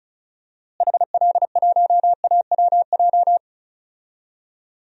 Morse code, heard as a keyed tone, about 34 wpm, 700 Hz. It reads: HL1AWJ
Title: DXCC-100 Call Signs - 6 Characters - Repeated 2x - 35wpm